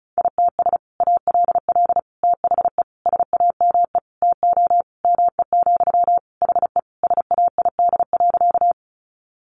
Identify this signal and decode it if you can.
Morse code, heard as a keyed tone, about 34 wpm, 700 Hz. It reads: ITS ALL THE SAME TO ME, HE SAID.
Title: Sentences from Top 600 Words 35wpm